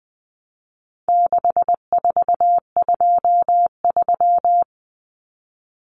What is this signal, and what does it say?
Morse code, heard as a keyed tone, about 20 wpm, 700 Hz. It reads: 6423
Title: Numbers - 4-Digit 20wpm